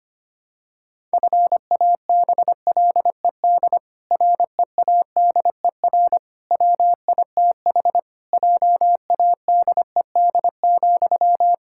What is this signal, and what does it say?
Morse code, heard as a keyed tone, about 25 wpm, 700 Hz. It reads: FABLED READER WITH JADED,
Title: Pangram Challenge v2 - 25wpm